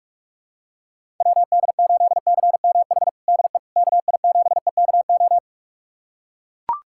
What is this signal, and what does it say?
Morse code, heard as a keyed tone, about 42 wpm, 700 Hz. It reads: WD8CMH DE KI7EKO
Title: USA Callsign DE Callsign 45wpm